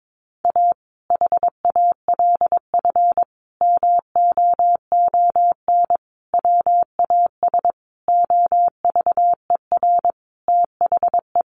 Morse code, heard as a keyed tone, about 22 wpm, 700 Hz. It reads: A HALF MOON WAS OVER THE
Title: Sentences from Top 1100 Words 22wpm